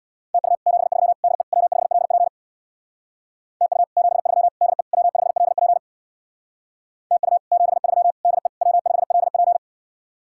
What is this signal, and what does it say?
Morse code, heard as a keyed tone, about 66 wpm, 700 Hz. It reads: TU 73 DE W5LP TU 73 DE W5LP TU 73 DE W5LP